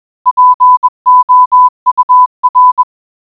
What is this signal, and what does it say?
Morse code, heard as a keyed tone, about 21 wpm, 1000 Hz. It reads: POUR